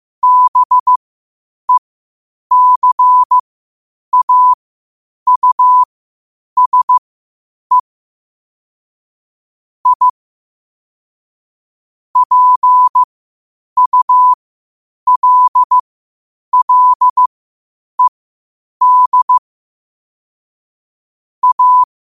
Morse code, heard as a keyed tone, about 15 wpm, spaced as for 8 wpm, 1000 Hz. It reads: BECAUSE I PULLED A
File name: qotdfiles / 8wpm